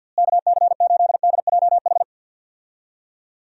Morse code, heard as a keyed tone, about 46 wpm, 700 Hz. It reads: KC8DJH
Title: USA Call Signs - 6 Characters - Repeated 3x - 50wpm